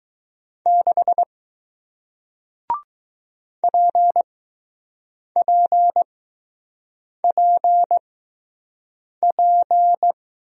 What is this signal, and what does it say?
Morse code, heard as a keyed tone, about 21 wpm, spaced as for 6 wpm, 700 Hz. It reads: 6 PPPP